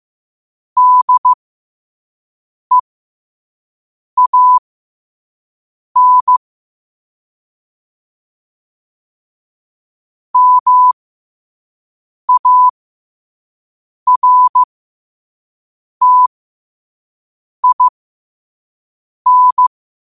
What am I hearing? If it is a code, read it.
Morse code, heard as a keyed tone, about 15 wpm, spaced as for 5 wpm, 1000 Hz. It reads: DEAN MARTIN
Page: New quotes every day in morse code at 5 Words per minute.